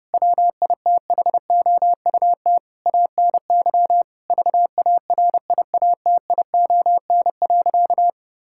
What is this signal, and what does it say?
Morse code, heard as a keyed tone, about 32 wpm, 700 Hz. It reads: WITHOUT ANY VARIATION.
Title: Sentences from Top 1600 Words 30wpm